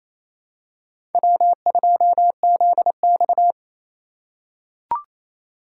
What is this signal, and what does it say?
Morse code, heard as a keyed tone, about 27 wpm, 700 Hz. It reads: W2ZX
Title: USA Call Signs - 3x Character Spacing - 28wpm